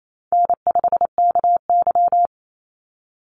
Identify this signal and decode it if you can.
Morse code, heard as a keyed tone, about 28 wpm, 700 Hz. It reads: N5KY
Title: Easy Call Signs - 28wpm